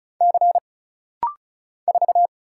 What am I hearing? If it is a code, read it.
Morse code, heard as a keyed tone, about 35 wpm, 700 Hz. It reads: C 4
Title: Single Letter-Number Alt - Rapid-Fire 35wpm